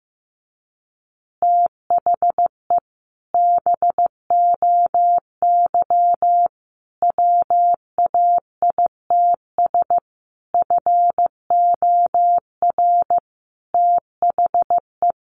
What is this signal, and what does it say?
Morse code, heard as a keyed tone, about 15 wpm, 700 Hz. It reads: THE BOY WAITS FOR THE
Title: Additional Easy Phrases 15wpm